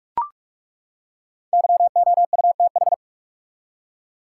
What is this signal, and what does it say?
Morse code, heard as a keyed tone, about 42 wpm, 700 Hz. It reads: YOUTH
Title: Five Letter Words - 45wpm